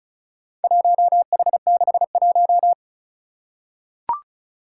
Morse code, heard as a keyed tone, about 36 wpm, 700 Hz. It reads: 1H61
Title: Four Challenging Letter-Number Combinations 35wpm